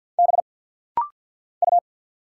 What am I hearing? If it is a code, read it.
Morse code, heard as a keyed tone, about 50 wpm, 700 Hz. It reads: B U
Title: Single Letter-Number Alt - Rapid-Fire 50wpm